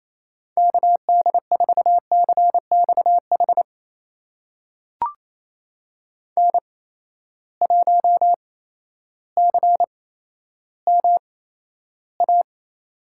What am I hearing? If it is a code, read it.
Morse code, heard as a keyed tone, about 28 wpm, spaced as for 13 wpm, 700 Hz. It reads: KD4CXH N1CMA
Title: USA Call Signs - 8x Character Spacing - 28wpm